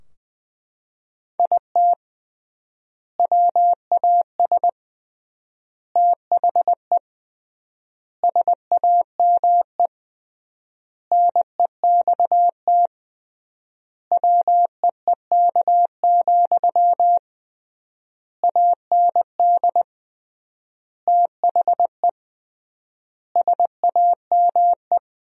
Morse code, heard as a keyed tone, about 20 wpm, 700 Hz. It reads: IT WAS THE SAME NEXT WEEK, AND THE SAME
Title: 3x Word-Spacing - Sentences from Top 500 Words - 20wpm